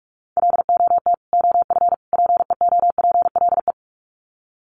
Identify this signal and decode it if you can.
Morse code, heard as a keyed tone, about 44 wpm, 700 Hz. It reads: LOT OF PEOPLE
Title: Sets of 3 Words 45wpm